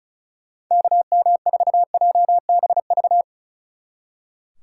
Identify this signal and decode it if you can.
Morse code, heard as a keyed tone, about 44 wpm, 700 Hz. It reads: KM4JBV